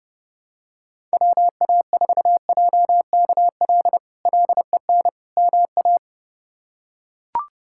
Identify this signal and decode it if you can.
Morse code, heard as a keyed tone, about 30 wpm, 700 Hz. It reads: WA4JKL LEN MA
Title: K1USN SST QSO Practice - 30wpm